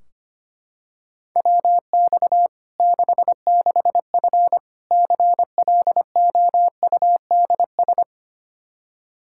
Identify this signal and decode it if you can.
Morse code, heard as a keyed tone, about 25 wpm, 700 Hz. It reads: WX 66F CLOUDS
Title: Practice QSO - Exchange 2 - Info - 25wpm